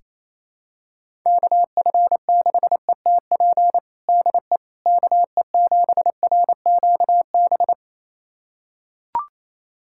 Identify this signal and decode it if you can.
Morse code, heard as a keyed tone, about 28 wpm, 700 Hz. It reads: KF6ETP DE KE7RQB
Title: USA Callsign DE Callsign - 6 Characters - 28wpm